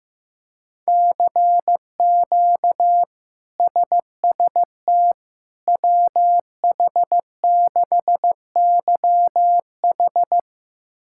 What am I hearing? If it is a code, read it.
Morse code, heard as a keyed tone, about 16 wpm, 700 Hz. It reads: CQ SST WH6YH
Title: K1USN SST QSO Practice - 15wpm